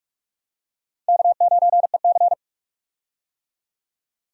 Morse code, heard as a keyed tone, about 46 wpm, 700 Hz. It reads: K9EC
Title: USA Call Signs - 4 Characters - 45wpm